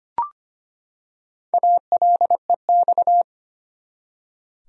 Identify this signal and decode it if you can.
Morse code, heard as a keyed tone, about 25 wpm, 700 Hz. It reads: ALEX
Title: USA Common Names - Max 4 Letters - 25wpm